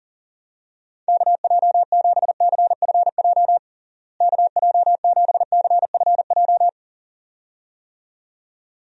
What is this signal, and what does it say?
Morse code, heard as a keyed tone, about 37 wpm, 700 Hz. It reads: KJ7CFJ KJ7CFJ
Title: USA Call Signs - 6 Characters - Repeated 2x - 40wpm